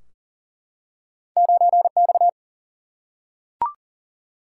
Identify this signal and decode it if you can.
Morse code, heard as a keyed tone, about 40 wpm, 700 Hz. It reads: 9 X
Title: Single-Letter Single-Number 40wpm